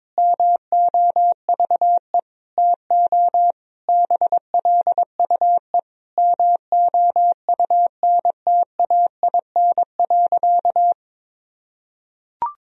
Morse code, heard as a keyed tone, about 21 wpm, 700 Hz. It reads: MOVE TO BLUE MOUNTAIN.
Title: Sentences from Top 1000 Words - Review Part 3 - 22wpm